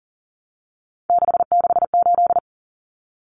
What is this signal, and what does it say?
Morse code, heard as a keyed tone, about 40 wpm, 700 Hz. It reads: 668
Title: Numbers - 3-Digit 40wpm